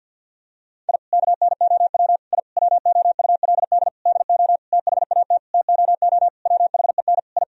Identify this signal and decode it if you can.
Morse code, heard as a keyed tone, about 47 wpm, 700 Hz. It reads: I KNOW I WOULD DO THAT TOO WHEN I
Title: Sentences from Top 1000 Words - Review Part 3 - 50wpm